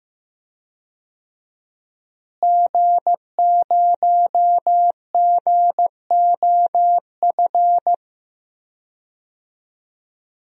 Morse code, heard as a keyed tone, about 16 wpm, 700 Hz. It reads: G0GOF
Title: DXCC-100 Call Signs - 5 Characters - Repeated 2x - 15wpm